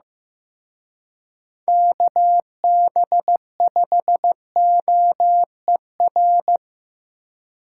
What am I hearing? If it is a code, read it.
Morse code, heard as a keyed tone, about 18 wpm, 700 Hz. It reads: KB5OER
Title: Speed Racing - USA 2x3 Call Signs 15wpm